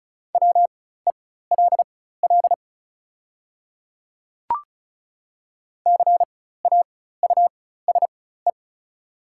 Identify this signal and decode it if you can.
Morse code, heard as a keyed tone, about 35 wpm, spaced as for 16 wpm, 690 Hz. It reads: WELL CAUSE
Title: 4x Character Spacing - Top 200 Words - 35wpm